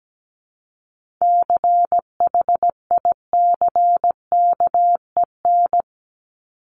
Morse code, heard as a keyed tone, about 17 wpm, 700 Hz. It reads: CHICKEN
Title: Multi-letter Phonemes Repeated 3x 17wpm